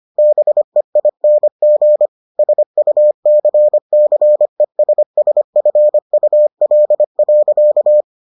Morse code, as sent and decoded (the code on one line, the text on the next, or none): -... .. -. --. / ... ..- -.-. -.-. ... ... ..-. ..- .-.. .-.-.-
BEING SUCCESSFUL.